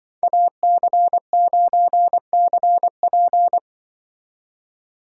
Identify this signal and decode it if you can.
Morse code, heard as a keyed tone, about 23 wpm, 700 Hz. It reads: AC9CP
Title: Speed Racing - USA Call Signs 20wpm